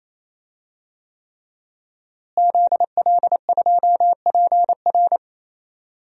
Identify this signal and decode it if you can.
Morse code, heard as a keyed tone, about 28 wpm, 700 Hz. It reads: ZL2PR
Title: DXCC-100 Call Signs - 5 Characters - Repeated 3x - 28wpm